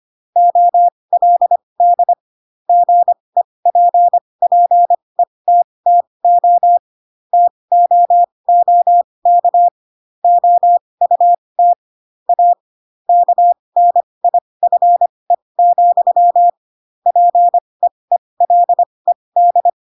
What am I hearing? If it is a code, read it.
Morse code, heard as a keyed tone, about 25 wpm, spaced as for 20 wpm, 700 Hz. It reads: OLD GEPPETTO TOOK OUT A KNIFE, PEELED
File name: Pinocchio / Pinocchio ch7 15 25 ik1hge